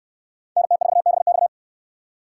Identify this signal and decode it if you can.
Morse code, heard as a keyed tone, about 59 wpm, 700 Hz. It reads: NT4BX